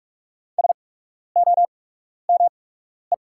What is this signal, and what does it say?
Morse code, heard as a keyed tone, about 44 wpm, spaced as for 12 wpm, 700 Hz. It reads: SOME
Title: 8x Character Spacing - Top 100 Words - 45wpm